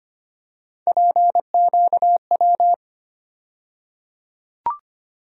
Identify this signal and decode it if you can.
Morse code, heard as a keyed tone, about 25 wpm, 700 Hz. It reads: PQW
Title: USA Call Sign Postfixes - 3 Letters 25wpm